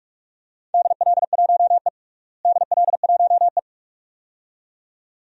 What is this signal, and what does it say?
Morse code, heard as a keyed tone, about 43 wpm, 700 Hz. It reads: DL1E DL1E
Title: DXCC-100 Call Signs - 4 Characters - Repeated 2x - 45wpm